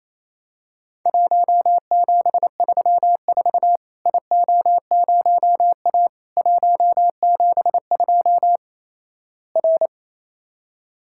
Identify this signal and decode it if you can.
Morse code, heard as a keyed tone, about 28 wpm, 700 Hz. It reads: 1734 IO0A 172 R
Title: Worked All Europe DX Contest - QTC Exchange - 28wpm